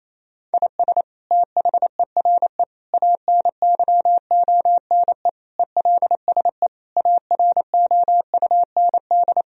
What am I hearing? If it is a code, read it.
Morse code, heard as a keyed tone, about 28 wpm, 700 Hz. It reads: IS THERE ANYONE ELSE AROUND
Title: Sentences from Top 1000 Words - Review Part 3 - 28wpm